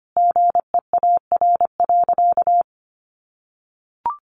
Morse code, heard as a keyed tone, about 25 wpm, 700 Hz. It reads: GEAR.
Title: Sentences from Top 1400 Words 25wpm